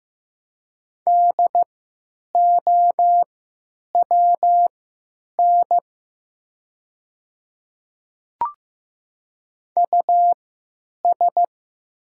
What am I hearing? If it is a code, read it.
Morse code, heard as a keyed tone, about 15 wpm, spaced as for 9 wpm, 700 Hz. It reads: DOWN US
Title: 3x Character Spacing - Top 300 Words - 15wpm